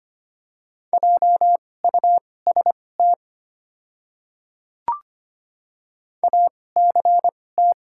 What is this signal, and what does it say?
Morse code, heard as a keyed tone, about 25 wpm, spaced as for 18 wpm, 700 Hz. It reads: JUST ACT
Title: 2x Character Spacing - Top 200 Words - 25wpm